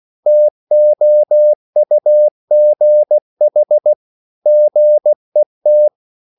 Morse code, heard as a keyed tone, about 16 wpm, 600 Hz. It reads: TOUGH GET
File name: CW / Practice / EnglishSayings16